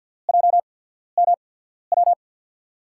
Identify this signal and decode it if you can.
Morse code, heard as a keyed tone, about 53 wpm, spaced as for 14 wpm, 700 Hz. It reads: JMW
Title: USA Call Signs - 6 Characters - 8x Character Spacing - 50wpm